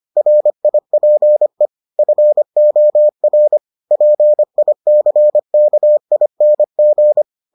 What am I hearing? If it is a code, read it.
Morse code, heard as a keyed tone, about 25 wpm, 600 Hz. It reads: RIPE FOR PICKING